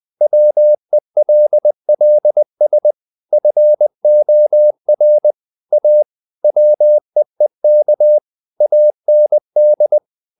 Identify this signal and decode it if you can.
Morse code, heard as a keyed tone, about 20 wpm, 600 Hz. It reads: WELLS FOR A WEEK AND
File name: the old man and the sea / the old man and the sea part 1 20wpm